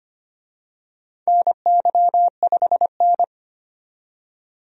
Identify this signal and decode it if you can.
Morse code, heard as a keyed tone, about 25 wpm, 700 Hz. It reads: NY5N
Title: USA Call Signs - 4 Characters - 25wpm